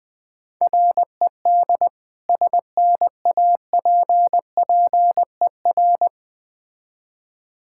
Morse code, heard as a keyed tone, about 20 wpm, 700 Hz. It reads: RED SNAPPER
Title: Difficult Words with Hints - 20wpm